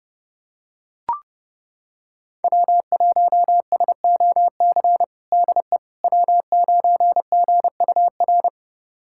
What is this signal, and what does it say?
Morse code, heard as a keyed tone, about 30 wpm, 700 Hz. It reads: W1SOC DE W9GUR
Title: USA Callsign DE Callsign - 5 Characters - 30wpm